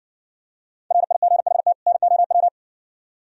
Morse code, heard as a keyed tone, about 60 wpm, 700 Hz. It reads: RIGHT NOW